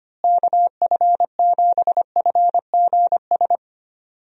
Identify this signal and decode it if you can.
Morse code, heard as a keyed tone, about 25 wpm, 700 Hz. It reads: KF7FGS
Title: USA Call Signs - 6 Characters - Repeated 3x - 25wpm